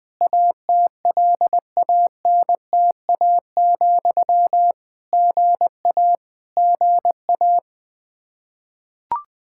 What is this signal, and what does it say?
Morse code, heard as a keyed tone, about 20 wpm, 700 Hz. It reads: ATLANTA, GA GA
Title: Practice QSO - Exchange 1 - Beginning - 20wpm